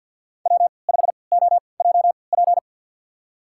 Morse code, heard as a keyed tone, about 49 wpm, spaced as for 28 wpm, 690 Hz. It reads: W5OJP
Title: USA Call Signs - 5 Characters - 3x Character Spacing - 50wpm